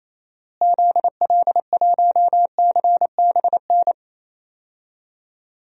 Morse code, heard as a keyed tone, about 28 wpm, 700 Hz. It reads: ZL1CBN
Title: DXCC-100 Call Signs - 6 Characters - Repeated 3x - 28wpm